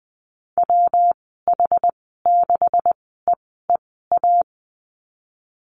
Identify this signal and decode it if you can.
Morse code, heard as a keyed tone, about 19 wpm, spaced as for 14 wpm, 700 Hz. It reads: WH6EEA
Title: 147 Lesson 24g - Letter y - Call Signs 20wpm